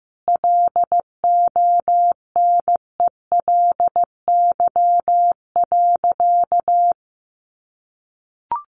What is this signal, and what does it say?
Morse code, heard as a keyed tone, about 15 wpm, 700 Hz. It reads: LONELY.
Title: Easy Phrases 15wpm